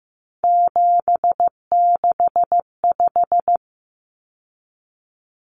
Morse code, heard as a keyed tone, about 15 wpm, 700 Hz. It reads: 765
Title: Numbers - 3-Digit 15wpm